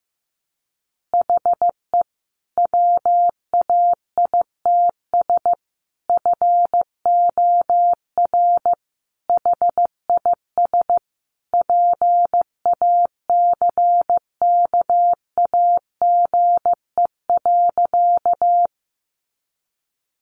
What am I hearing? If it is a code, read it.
Morse code, heard as a keyed tone, about 15 wpm, 700 Hz. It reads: HE WAITS FOR HIS PACKAGE.
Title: Additional Easy Phrases 15wpm